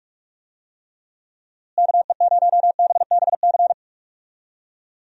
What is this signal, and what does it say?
Morse code, heard as a keyed tone, about 42 wpm, 700 Hz. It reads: KE0BBC
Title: USA Call Signs - 6 Characters - Repeated 2x - 45wpm